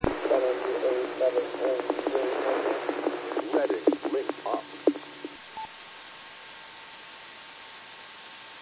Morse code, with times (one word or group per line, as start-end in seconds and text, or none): none